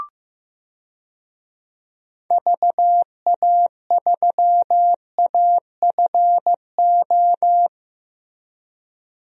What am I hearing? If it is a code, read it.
Morse code, heard as a keyed tone, about 15 wpm, 700 Hz. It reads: VA3AFO
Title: DXCC-100 Call Signs - 6 Characters - 15wpm